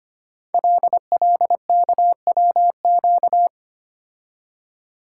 Morse code, heard as a keyed tone, about 25 wpm, 700 Hz. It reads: LLKWQ
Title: Code Groups - 5 Letter:Numbers - 25wpm